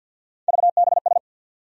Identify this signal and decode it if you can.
Morse code, heard as a keyed tone, about 47 wpm, 700 Hz. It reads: VBS
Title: USA Call Sign Postfixes - 3 Letters 50wpm